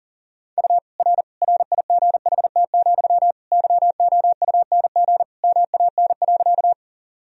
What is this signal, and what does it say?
Morse code, heard as a keyed tone, about 40 wpm, 700 Hz. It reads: U R RIGHT, YOUNG MAN.
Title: Sentences from Top 400 Words and Abbreviations 40wpm